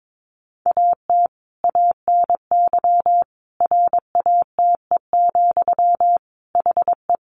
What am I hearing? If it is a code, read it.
Morse code, heard as a keyed tone, about 22 wpm, 700 Hz. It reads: AT ANY RATE, HE
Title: Sentences from Top 900 Words 22wpm